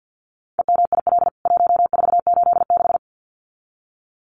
Morse code, heard as a keyed tone, about 48 wpm, 700 Hz. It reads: EMIL 1486
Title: CWT Exchange 50wpm